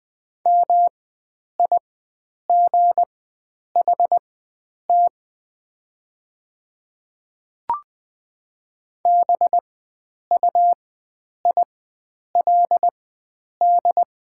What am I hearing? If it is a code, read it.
Morse code, heard as a keyed tone, about 20 wpm, spaced as for 9 wpm, 700 Hz. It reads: MIGHT BUILD
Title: 4x Character Spacing - Top 300 Words - 20wpm